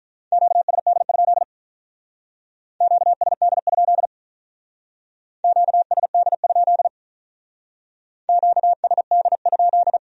Morse code, heard as a keyed tone, about 42 wpm, 700 Hz. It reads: QSD? QSD? QSD? QSD?
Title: Speed-Racing Q-Codes - 35wpm